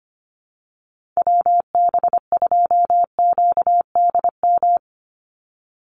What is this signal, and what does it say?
Morse code, heard as a keyed tone, about 25 wpm, 700 Hz. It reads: WB2QDM
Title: USA Call Signs 25wpm